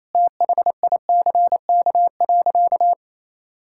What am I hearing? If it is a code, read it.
Morse code, heard as a keyed tone, about 27 wpm, 700 Hz. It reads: THICK.
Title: Sentences from Top 1600 Words 28wpm